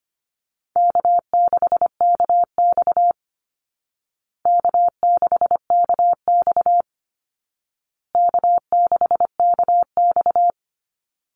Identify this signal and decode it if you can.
Morse code, heard as a keyed tone, about 25 wpm, 700 Hz. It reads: K6KX K6KX K6KX
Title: Call Signs Repeated 2x - 25wpm